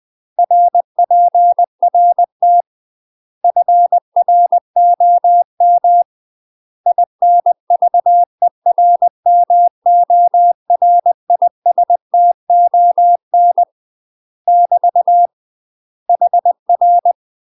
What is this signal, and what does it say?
Morse code, heard as a keyed tone, about 20 wpm, 700 Hz. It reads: RPRT FROM INVERMORISTON <BT> HR
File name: UK MASTER QSOs TRAINING / 20WPM / UK MASTER QSOs 0043